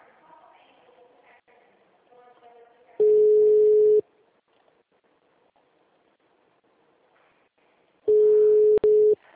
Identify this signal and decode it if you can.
Ringback tone (Europe, 425 Hz).
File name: out- -2002- - - .64515